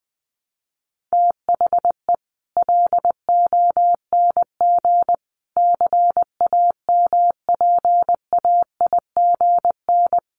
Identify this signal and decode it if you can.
Morse code, heard as a keyed tone, about 20 wpm, 700 Hz. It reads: THE LONG CAMPAIGN
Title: Sentences from Top 1200 Words 20wpm